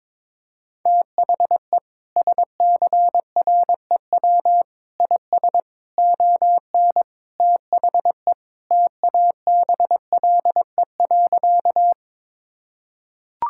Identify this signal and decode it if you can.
Morse code, heard as a keyed tone, about 22 wpm, 700 Hz. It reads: THE SCREW IS ON THE TABLE.
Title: Sentences from Top 1600 Words 22wpm